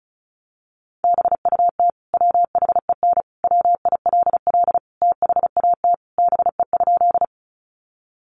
Morse code, heard as a keyed tone, about 34 wpm, 700 Hz. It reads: BUT WHEN WILL THAT BE?
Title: Sentences from Top 100 Words 35wpm